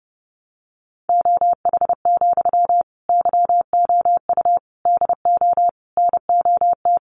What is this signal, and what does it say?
Morse code, heard as a keyed tone, about 30 wpm, 700 Hz. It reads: OH, YOU DO NOT
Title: Sentences from Top 600 Words 30wpm